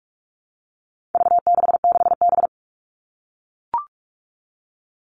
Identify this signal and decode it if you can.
Morse code, heard as a keyed tone, about 44 wpm, 700 Hz. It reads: V66B
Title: All Cut Numbers - 4-Digit 45wpm